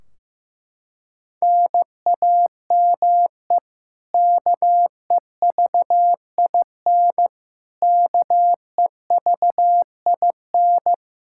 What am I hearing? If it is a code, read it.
Morse code, heard as a keyed tone, about 15 wpm, 700 Hz. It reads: NAME KEVIN KEVIN
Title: Practice QSO - Exchange 1 - Beginning - 15wpm